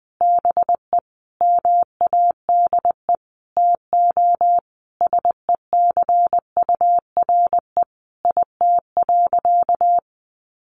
Morse code, heard as a keyed tone, about 20 wpm, 700 Hz. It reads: BE MADE TO SECURE IT.
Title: Sentences from Top 1200 Words 20wpm